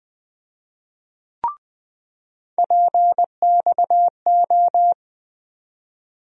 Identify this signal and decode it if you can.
Morse code, heard as a keyed tone, about 20 wpm, 700 Hz. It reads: PXO
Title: USA Call Sign Postfixes - 3 Letters 20wpm